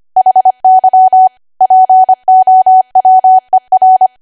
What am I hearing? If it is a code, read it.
Morse code, heard as a keyed tone, about 25 wpm, 750 Hz. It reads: HY POWER
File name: Archive / 25 WPM / 25WPM